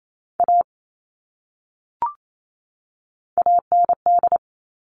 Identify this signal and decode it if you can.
Morse code, heard as a keyed tone, about 28 wpm, 700 Hz. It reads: A AND
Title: Top 500 Words 28wpm